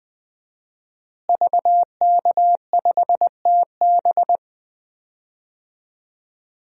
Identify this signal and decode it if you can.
Morse code, heard as a keyed tone, about 20 wpm, 700 Hz. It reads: VK5TB
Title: DXCC-100 Call Signs - 5 Characters - Repeated 3x - 20wpm